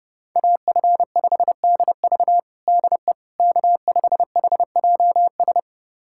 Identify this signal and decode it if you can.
Morse code, heard as a keyed tone, about 29 wpm, 700 Hz. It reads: AF5DV DE K5HJS
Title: USA Callsign DE Callsign - 5 Characters - 30wpm